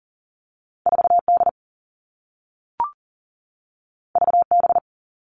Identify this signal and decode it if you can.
Morse code, heard as a keyed tone, about 40 wpm, 700 Hz. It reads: 4D VB
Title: All Cut Numbers - 2-Digit 40wpm